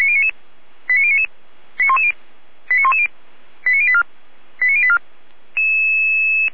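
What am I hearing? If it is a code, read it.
Five-tone selective call (ZVEI), heard as quick runs of five tones, tones of about 70 ms. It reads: 89099 89110 89084 89084
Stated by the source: ZVEI five-tone call received on SDR.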